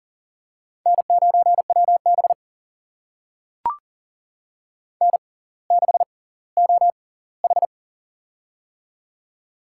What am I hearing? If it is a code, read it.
Morse code, heard as a keyed tone, about 40 wpm, spaced as for 20 wpm, 700 Hz. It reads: N9WB N6OH
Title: USA Call Signs - 4 Characters - 6x Character Spacing - 40wpm